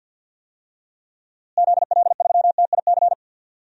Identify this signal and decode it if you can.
Morse code, heard as a keyed tone, about 53 wpm, 700 Hz. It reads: ZL3TIC
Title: DXCC-100 Call Signs - 6 Characters - Repeated 2x - 50wpm